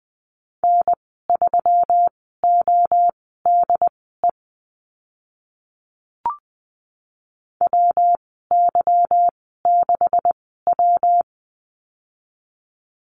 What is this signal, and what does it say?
Morse code, heard as a keyed tone, about 20 wpm, spaced as for 15 wpm, 700 Hz. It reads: N3ODE WY6W
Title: 147 Lesson 24g - Letter y - Call Signs 20wpm